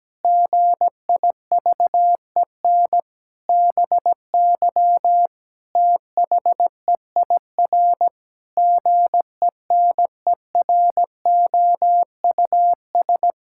Morse code, heard as a keyed tone, about 17 wpm, 700 Hz. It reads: GIVEN BY THEIR GENEROUS
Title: Pangram Challenge v2 - 17wpm